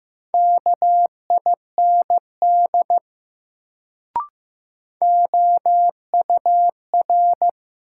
Morse code, heard as a keyed tone, about 15 wpm, 700 Hz. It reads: KIND OUR
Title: Sets of 2 Words - New Word Drill - 15wpm